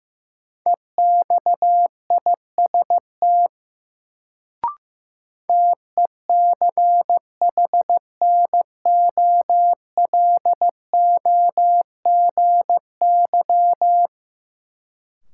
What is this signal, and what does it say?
Morse code, heard as a keyed tone, about 15 wpm, 700 Hz. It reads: EXIST TECHNOLOGY
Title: Sentences from Top 1000 Words - Review Part 5 - New Word Drill - 15wpm